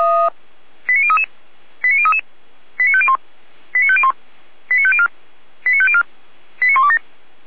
Five-tone selective call (ZVEI), heard as quick runs of five tones, tones of about 70 ms. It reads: 89033 89033 89501 89501 89504 89504 89127